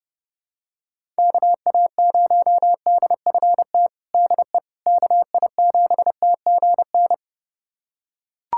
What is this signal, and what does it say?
Morse code, heard as a keyed tone, about 28 wpm, 700 Hz. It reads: KA0DFT DE KI7TGN
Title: USA Callsign DE Callsign - 6 Characters - 30wpm